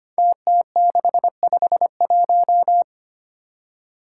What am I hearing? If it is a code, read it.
Morse code, heard as a keyed tone, about 25 wpm, 700 Hz. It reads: TT651
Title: Five Challenging Letter-Number Combinations 25wpm